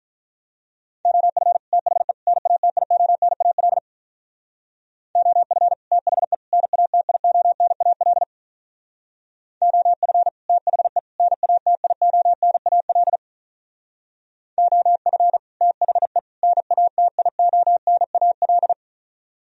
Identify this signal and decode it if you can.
Morse code, heard as a keyed tone, about 43 wpm, 700 Hz. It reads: OF THE NATIONAL OF THE NATIONAL OF THE NATIONAL OF THE NATIONAL
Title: Speed-Racing - Sets of 3 Words 35wpm